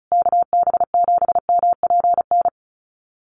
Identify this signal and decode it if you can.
Morse code, heard as a keyed tone, about 34 wpm, 700 Hz. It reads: KB7MPN
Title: USA Call Signs Repeated 3x 35wpm